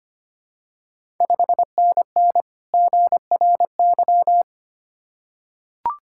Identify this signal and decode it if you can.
Morse code, heard as a keyed tone, about 25 wpm, 700 Hz. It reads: 5NN GRY
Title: Ontario QSO Party Exchange 25wpm